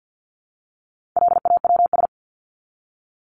Morse code, heard as a keyed tone, about 50 wpm, 700 Hz. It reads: LAWS
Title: ES:S Suffix - Top 100 Words with Base Words 50wpm